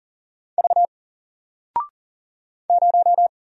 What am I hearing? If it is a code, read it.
Morse code, heard as a keyed tone, about 40 wpm, 700 Hz. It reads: V 0
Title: Single Character (Letter:Number:Punctuation:Prosign) 40wpm